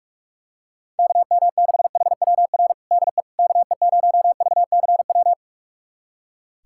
Morse code, heard as a keyed tone, about 41 wpm, 700 Hz. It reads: KM6HWR DE KE0VCW
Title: USA Callsign DE Callsign - 6 Characters - 45wpm